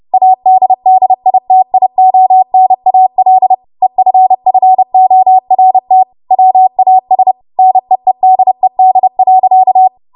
Morse code, heard as a keyed tone, about 30 wpm, 750 Hz. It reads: ADDITIONAL EFFORT WAS NEEDED.
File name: Morse / 30WPM